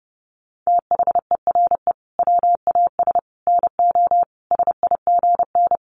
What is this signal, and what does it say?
Morse code, heard as a keyed tone, about 30 wpm, 700 Hz. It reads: THERE WAS NO SIGN
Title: Sets of 5 Words Encore2 30wpm